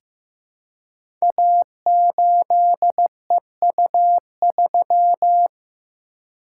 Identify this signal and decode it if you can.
Morse code, heard as a keyed tone, about 15 wpm, 700 Hz. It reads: A8EU3
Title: Code Groups - 5 Letter:Numbers - 15wpm